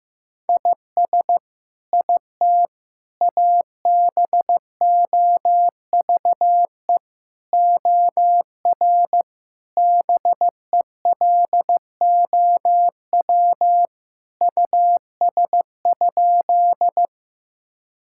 Morse code, heard as a keyed tone, about 15 wpm, 700 Hz. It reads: IS IT ABOVE OR BELOW US?
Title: Sentences from Top 1000 Words - Review Part 5 - 15wpm